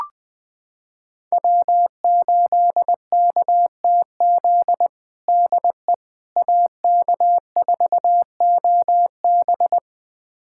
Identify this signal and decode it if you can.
Morse code, heard as a keyed tone, about 20 wpm, 700 Hz. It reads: W8KTZ DE AK4OB
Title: USA Callsign DE Callsign - 5 Characters - 20wpm